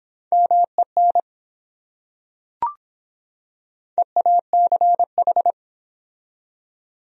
Morse code, heard as a keyed tone, about 24 wpm, 700 Hz. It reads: MEN EACH